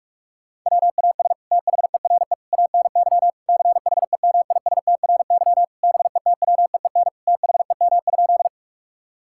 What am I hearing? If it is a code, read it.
Morse code, heard as a keyed tone, about 46 wpm, 690 Hz. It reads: WAS THERE ANY CHEMISTRY BETWEEN THEM?
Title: Sentences from Top 1000 Words - Review Part 6 - 45wpm